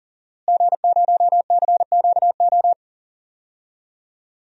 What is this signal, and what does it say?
Morse code, heard as a keyed tone, about 40 wpm, 700 Hz. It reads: G0CQO
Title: DXCC-100 Call Signs - 5 Characters - Repeated 2x - 40wpm